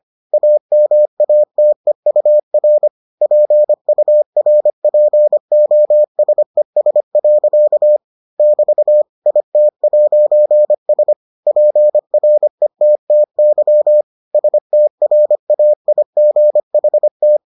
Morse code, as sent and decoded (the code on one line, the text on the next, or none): .- -- .- - . ..- .-. / .--. ..- .-. .--. --- ... ... .-.-.- / -...- / .. - .----. ... / .--. .-. - - -.-- / ... - .-. .- .. --. .... -
AMATEUR PURPOSES. <BT> IT'S PRETTY STRAIGHT